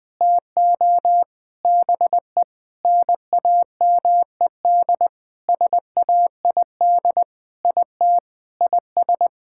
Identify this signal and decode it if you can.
Morse code, heard as a keyed tone, about 20 wpm, 700 Hz. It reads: TO BE NAMED SAID IT IS
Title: Sentences with 7-Word Phrases - 20wpm